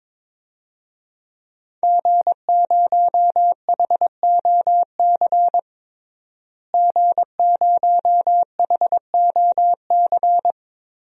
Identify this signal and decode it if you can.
Morse code, heard as a keyed tone, about 22 wpm, 700 Hz. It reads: G0HOC G0HOC
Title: DXCC-100 Call Signs - 5 Characters - Repeated 3x - 22wpm